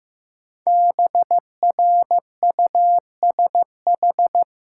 Morse code, heard as a keyed tone, about 15 wpm, 700 Hz. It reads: BRUSH
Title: Sentences from Top 1100 Words - New Word Drill Alt - 15wpm